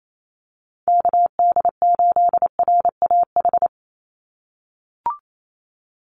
Morse code, heard as a keyed tone, about 28 wpm, 700 Hz. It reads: KD8RAH
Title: USA Call Signs 28wpm